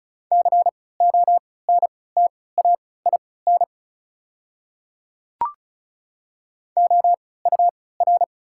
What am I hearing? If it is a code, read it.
Morse code, heard as a keyed tone, about 36 wpm, spaced as for 20 wpm, 700 Hz. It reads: CONTAIN OUR
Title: 3x Character Spacing - Top 500 Words - 35wpm